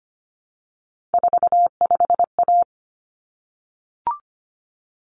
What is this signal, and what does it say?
Morse code, heard as a keyed tone, about 25 wpm, 700 Hz. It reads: 45A
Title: Common Cut Numbers - 3-Digit 25wpm